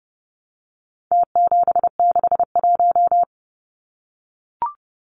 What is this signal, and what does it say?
Morse code, heard as a keyed tone, about 30 wpm, 700 Hz. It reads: T761
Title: Numbers - Contest Style - 4-Digit 30wpm